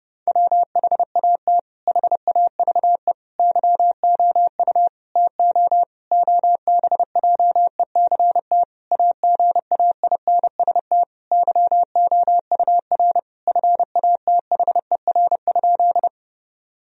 Morse code, heard as a keyed tone, about 32 wpm, 700 Hz. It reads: WHAT HAVE YOU TO OBJECT AGAINST YOUR FATHER?
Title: Sentences from Top 1000 Words - Review Part 3 - 30wpm